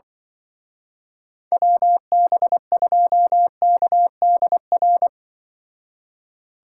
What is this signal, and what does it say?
Morse code, heard as a keyed tone, about 24 wpm, 700 Hz. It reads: WB2KDR
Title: Speed Racing - USA 2x3 Call Signs 20wpm